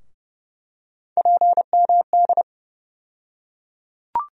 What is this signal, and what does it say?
Morse code, heard as a keyed tone, about 31 wpm, 700 Hz. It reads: PMD
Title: USA Call Sign Postfixes - 3 Letters 30wpm